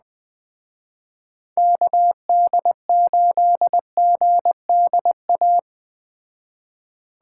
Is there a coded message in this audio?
Yes: Morse code from 1.6 to 5.6 s, heard as a keyed tone, about 19 wpm, 700 Hz. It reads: KD8GDA